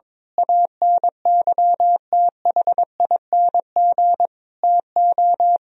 Morse code, heard as a keyed tone, about 21 wpm, 700 Hz. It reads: ANYTHING TO
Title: Sentences from Top 1000 Words - Review Part 1 - 22wpm